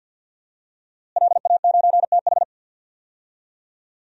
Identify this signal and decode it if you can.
Morse code, heard as a keyed tone, about 50 wpm, 700 Hz. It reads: LA9TH
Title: DXCC-100 Call Signs - 5 Characters - Repeated 3x - 50wpm